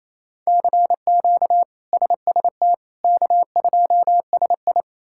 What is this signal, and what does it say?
Morse code, heard as a keyed tone, about 28 wpm, 700 Hz. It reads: CQ SST K2SI
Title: K1USN SST QSO Practice - 28wpm